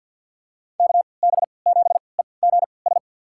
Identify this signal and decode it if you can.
Morse code, heard as a keyed tone, about 47 wpm, spaced as for 27 wpm, 700 Hz. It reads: KB7EGS
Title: USA Call Signs - 6 Characters - 3x Character Spacing - 50wpm